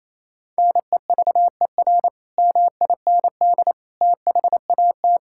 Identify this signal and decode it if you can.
Morse code, heard as a keyed tone, about 28 wpm, 700 Hz. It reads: NEVER MIND THAT
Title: Sentences from Top 1000 Words - Review Part 5 - 28wpm